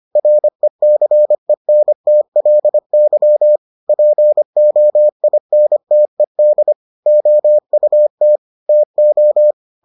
Morse code, heard as a keyed tone, about 25 wpm, 600 Hz. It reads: RECENTLY POINTED OUT TO